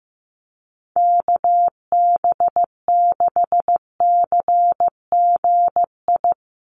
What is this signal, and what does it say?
Morse code, heard as a keyed tone, about 15 wpm, 700 Hz. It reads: KB6CGI
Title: USA Call Signs 15wpm